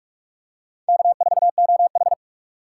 Morse code, heard as a keyed tone, about 46 wpm, 700 Hz. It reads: K4OH
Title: USA Call Signs - 4 Characters - 6x Character Spacing - 45wpm